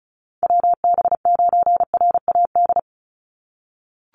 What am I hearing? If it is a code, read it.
Morse code, heard as a keyed tone, about 34 wpm, 700 Hz. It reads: WB9RAD
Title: USA Call Signs 35wpm